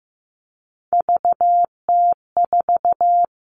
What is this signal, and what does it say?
Morse code, heard as a keyed tone, about 15 wpm, 700 Hz. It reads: VT4
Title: All Cut Numbers - 3-Digit 15wpm